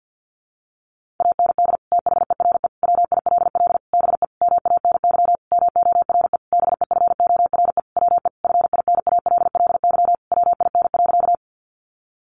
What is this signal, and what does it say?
Morse code, heard as a keyed tone, about 50 wpm, 700 Hz. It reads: AND THERE WILL BE MANY MORE BEFORE WE FINALLY WIN.
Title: Sentences from Top 1500 Words 50wpm